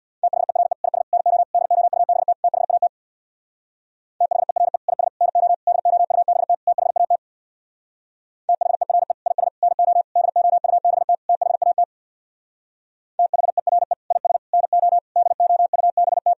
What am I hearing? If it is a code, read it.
Morse code, heard as a keyed tone, about 64 wpm, 700 Hz. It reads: THERE IS NO DOUBT THAT THERE IS NO DOUBT THAT THERE IS NO DOUBT THAT THERE IS NO DOUBT